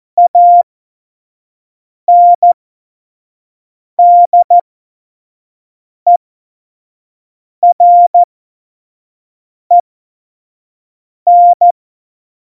Morse code, heard as a keyed tone, about 14 wpm, spaced as for 5 wpm, 700 Hz. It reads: ANDEREN
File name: Morsekurs / Witze / Witz 14 April 2020, 25 F70